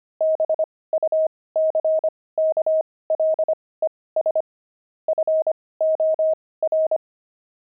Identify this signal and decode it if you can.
Morse code, heard as a keyed tone, about 25 wpm, spaced as for 18 wpm, 630 Hz. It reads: BUCKLES FOR
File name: pangram-fs-18-cs-25-buckles